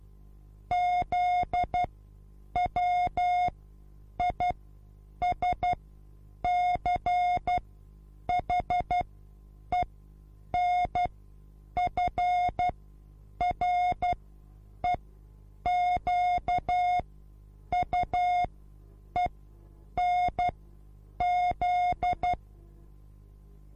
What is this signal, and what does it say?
Morse code, heard as a keyed tone, about 12 wpm, spaced as for 8 wpm, 720 Hz. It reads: ZWISCHENFREQUENZ